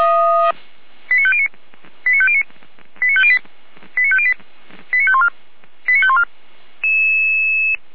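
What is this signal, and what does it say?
Five-tone selective call (ZVEI), heard as quick runs of five tones, tones of about 70 ms. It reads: 89509 89509 89508 89508 89514 89514
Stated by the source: ZVEI five-tone call received by SDR, archived by Feuerwehr Orlamünde (fire brigade).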